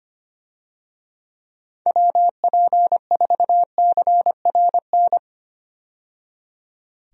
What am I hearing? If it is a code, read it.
Morse code, heard as a keyed tone, about 25 wpm, 700 Hz. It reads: WP4CRN
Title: USA Call Signs - 6 Characters - 25wpm